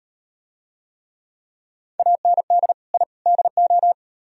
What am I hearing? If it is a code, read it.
Morse code, heard as a keyed tone, about 38 wpm, 700 Hz. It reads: AND I DO